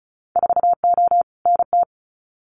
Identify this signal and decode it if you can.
Morse code, heard as a keyed tone, about 34 wpm, 700 Hz. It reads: 4O NT
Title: Winter Field Day Exchanges Encore 35wpm